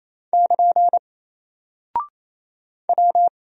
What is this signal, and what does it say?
Morse code, heard as a keyed tone, about 28 wpm, 700 Hz. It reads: <KN> W
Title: Single Character (Letter:Number:Punctuation:Prosign) 28wpm